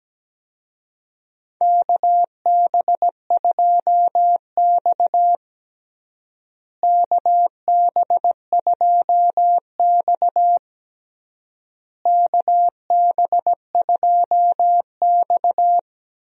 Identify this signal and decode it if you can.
Morse code, heard as a keyed tone, about 17 wpm, 700 Hz. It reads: KB2X KB2X KB2X
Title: USA Call Signs - 4 Characters - Repeated 3x - 17wpm